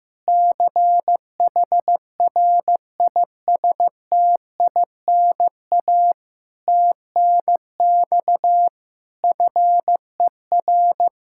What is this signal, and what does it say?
Morse code, heard as a keyed tone, about 15 wpm, 700 Hz. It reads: CHRISTINA TNX FER
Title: Practice QSO - Exchange 4 - Ending - 15wpm